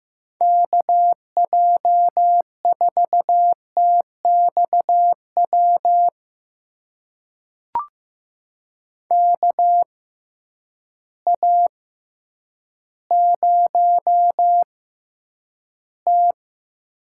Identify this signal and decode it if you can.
Morse code, heard as a keyed tone, about 15 wpm, spaced as for 9 wpm, 700 Hz. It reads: KJ4TXW KA0T
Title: USA Call Signs - 6x Character Spacing - 15wpm